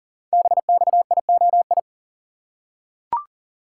Morse code, heard as a keyed tone, about 40 wpm, 700 Hz. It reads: BXIOI
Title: Code Groups - 5 Letters - 40wpm